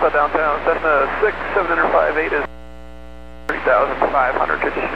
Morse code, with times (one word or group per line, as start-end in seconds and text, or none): none